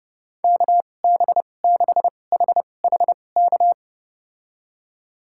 Morse code, heard as a keyed tone, about 32 wpm, spaced as for 22 wpm, 700 Hz. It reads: KB6HHK
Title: USA Call Signs - 6 Characters - 2x Character Spacing - 30wpm